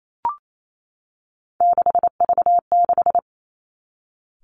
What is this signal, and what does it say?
Morse code, heard as a keyed tone, about 28 wpm, 700 Hz. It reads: 6V6
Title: Three Challenging Letter-Number Combinations 28wpm